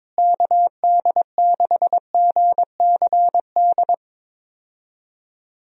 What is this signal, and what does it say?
Morse code, heard as a keyed tone, about 22 wpm, 700 Hz. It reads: KD6GCD
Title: USA Call Signs - 6 Characters - Repeated 2x - 22wpm